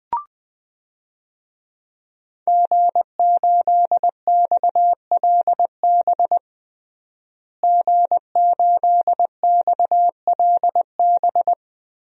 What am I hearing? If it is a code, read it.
Morse code, heard as a keyed tone, about 20 wpm, 700 Hz. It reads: G8XLB G8XLB
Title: DXCC-100 Call Signs - 5 Characters - Repeated 3x - 20wpm